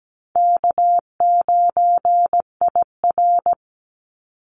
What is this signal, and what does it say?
Morse code, heard as a keyed tone, about 17 wpm, 700 Hz. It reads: K9IR
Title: Easy Call Signs Repeated 2x - 17wpm